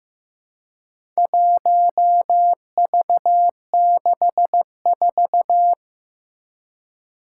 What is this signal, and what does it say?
Morse code, heard as a keyed tone, about 15 wpm, 700 Hz. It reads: 1V64
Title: Four Challenging Letter-Number Combinations 15wpm